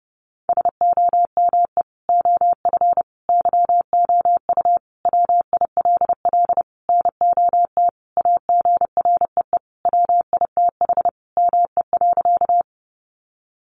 Morse code, heard as a keyed tone, about 30 wpm, 700 Hz. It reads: SOME OF YOU WILL NOT AGREE WITH ME.
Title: Sentences from Top 1100 Words 30wpm